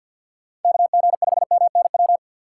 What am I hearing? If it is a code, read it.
Morse code, heard as a keyed tone, about 48 wpm, 700 Hz. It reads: KG5MNW
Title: USA Call Signs - 6 Characters - 6x Character Spacing - 50wpm